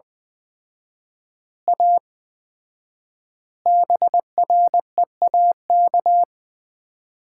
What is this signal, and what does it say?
Morse code, heard as a keyed tone, about 20 wpm, 700 Hz. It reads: A BREAK
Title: 4x Word-Spacing - Sets of 5 Words - 20wpm